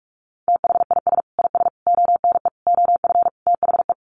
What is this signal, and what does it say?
Morse code, heard as a keyed tone, about 44 wpm, 700 Hz. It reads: THIS IS ONE OF THE
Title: Sets of 5 Words 45wpm